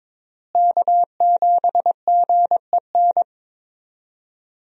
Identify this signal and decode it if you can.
Morse code, heard as a keyed tone, about 22 wpm, 700 Hz. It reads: K7GEN
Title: USA Call Signs - 5 Characters - Repeated 3x - 22wpm